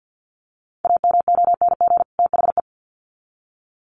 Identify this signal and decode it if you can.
Morse code, heard as a keyed tone, about 50 wpm, 700 Hz. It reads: AMONG THE
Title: Sets of 2 Words 50wpm